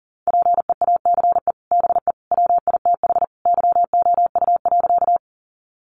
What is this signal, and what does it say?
Morse code, heard as a keyed tone, about 40 wpm, 700 Hz. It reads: PEACE BE WITH YOU.
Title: Sentences from Top 1500 Words 40wpm